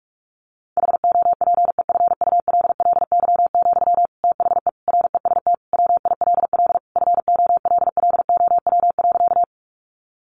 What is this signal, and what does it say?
Morse code, heard as a keyed tone, about 44 wpm, 700 Hz. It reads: HOPEFULLY, THE REST WILL FOLLOW.
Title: Sentences from Top 1500 Words 45wpm